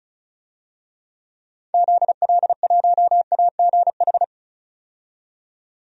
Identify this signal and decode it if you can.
Morse code, heard as a keyed tone, about 34 wpm, 700 Hz. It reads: ZL1AGH
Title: DXCC-100 Call Signs - 6 Characters - 35wpm